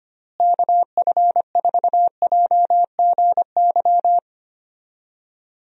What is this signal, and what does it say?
Morse code, heard as a keyed tone, about 25 wpm, 700 Hz. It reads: KF4JGY
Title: USA Call Signs - 6 Characters - Repeated 2x - 25wpm